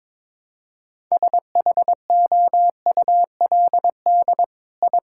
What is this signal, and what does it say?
Morse code, heard as a keyed tone, about 22 wpm, 700 Hz. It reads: SHOULD I
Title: Sentences from Top 1000 Words - Review Part 5 - 22wpm